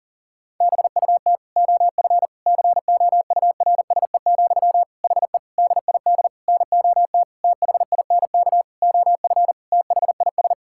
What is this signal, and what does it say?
Morse code, heard as a keyed tone, about 37 wpm, 700 Hz. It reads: BUT OF COURSE, HE DID NOT THINK OF THIS
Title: Sentences from Top 1600 Words 40wpm